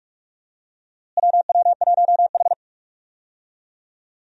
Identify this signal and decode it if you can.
Morse code, heard as a keyed tone, about 41 wpm, 700 Hz. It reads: WW1H
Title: USA Call Signs - 4 Characters - Repeated 3x - 45wpm